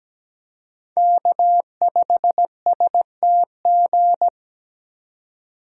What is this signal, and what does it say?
Morse code, heard as a keyed tone, about 20 wpm, 700 Hz. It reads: K5STG
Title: Speed Racing - USA 1x3 Call Signs 17wpm